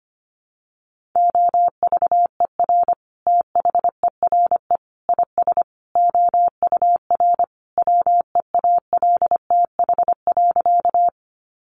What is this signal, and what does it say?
Morse code, heard as a keyed tone, about 25 wpm, 700 Hz. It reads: OVER THERE IS OUR WEALTH.
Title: Sentences from Top 1200 Words 25wpm